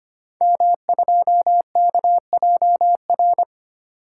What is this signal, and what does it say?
Morse code, heard as a keyed tone, about 25 wpm, 700 Hz. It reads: M2KJR
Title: Code Groups - 5 Letter:Numbers - 25wpm